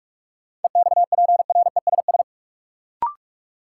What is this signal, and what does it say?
Morse code, heard as a keyed tone, about 47 wpm, 700 Hz. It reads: EXPRESS
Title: Sentences from Top 1100 Words - New Word Drill Alt - 45wpm